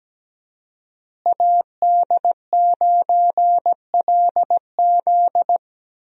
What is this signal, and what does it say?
Morse code, heard as a keyed tone, about 17 wpm, 700 Hz. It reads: AD9LZ
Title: USA Call Signs - 5 Characters - Repeated 3x - 17wpm